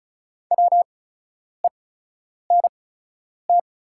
Morse code, heard as a keyed tone, about 36 wpm, spaced as for 10 wpm, 700 Hz. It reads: WENT
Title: 8x Character Spacing - Top 500 Words - 35wpm